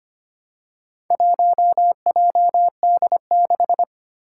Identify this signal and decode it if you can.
Morse code, heard as a keyed tone, about 25 wpm, 700 Hz. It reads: 1JD6
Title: Four Challenging Letter-Number Combinations 25wpm